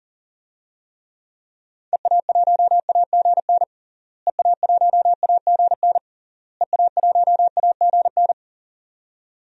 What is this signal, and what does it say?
Morse code, heard as a keyed tone, about 37 wpm, 700 Hz. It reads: EA1AGN EA1AGN EA1AGN
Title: DXCC-100 Call Signs - 6 Characters - Repeated 3x - 40wpm